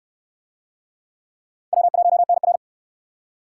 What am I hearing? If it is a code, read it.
Morse code, heard as a keyed tone, about 68 wpm, 700 Hz. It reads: W1NR